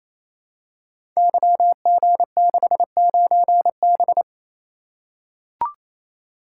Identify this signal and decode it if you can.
Morse code, heard as a keyed tone, about 29 wpm, 700 Hz. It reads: YG69B
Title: Code Groups - 5 Letter:Numbers - 28wpm